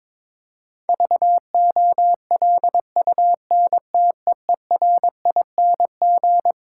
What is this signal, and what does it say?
Morse code, heard as a keyed tone, about 22 wpm, 700 Hz. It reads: VOLUNTEERING
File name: Difficult Sentences with Hints - 22wpm